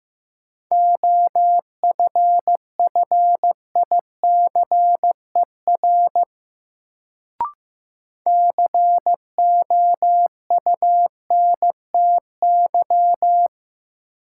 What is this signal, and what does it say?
Morse code, heard as a keyed tone, about 16 wpm, 700 Hz. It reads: OFFICER COUNTY
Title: Sentences from Top 1200 Words - New Word Drill Alt - 15wpm